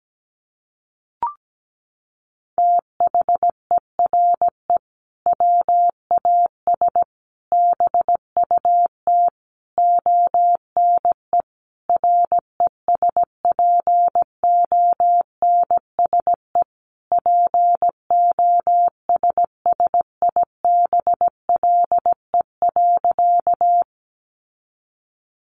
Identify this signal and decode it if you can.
Morse code, heard as a keyed tone, about 17 wpm, 700 Hz. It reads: THERE WAS BUT ONE RESPONSE POSSIBLE.
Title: Sentences from Top 1400 Words 17wpm